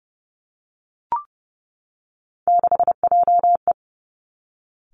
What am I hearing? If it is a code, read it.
Morse code, heard as a keyed tone, about 30 wpm, 700 Hz. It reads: 6JE
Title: Three Challenging Letter-Number Combinations 30wpm